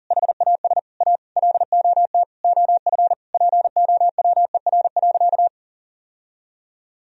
Morse code, heard as a keyed tone, about 40 wpm, 700 Hz. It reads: HAS A LOT OF POWER.
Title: Sentences from Top 1000 Words - Review Part 4 - 40wpm